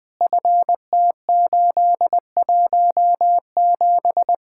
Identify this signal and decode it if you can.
Morse code, heard as a keyed tone, about 22 wpm, 700 Hz. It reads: FT817
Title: Speed-Racing Common QSO Phrases - 20wpm